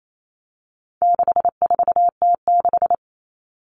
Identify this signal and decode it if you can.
Morse code, heard as a keyed tone, about 28 wpm, 700 Hz. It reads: 64T6
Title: Common Cut Numbers - 4-Digit 28wpm